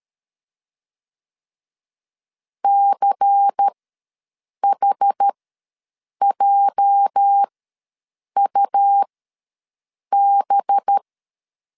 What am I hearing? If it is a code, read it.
Morse code, heard as a keyed tone, about 13 wpm, spaced as for 7 wpm, 800 Hz. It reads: CHJUB
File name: LEZIONI / 8 YJ45 / 8 YJ ws 12wpm 3342